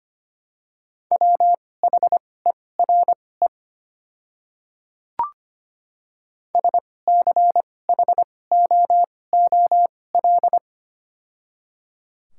Morse code, heard as a keyed tone, about 25 wpm, spaced as for 18 wpm, 700 Hz. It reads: WHERE SCHOOL
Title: 2x Character Spacing - Top 500 Words - 25wpm